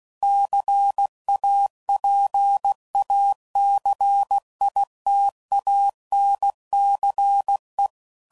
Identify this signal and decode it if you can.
Morse code, heard as a keyed tone, about 16 wpm, 800 Hz. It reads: CAPACITANCE
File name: CW Code Groups @15WPM / Technical Subjects / Reactance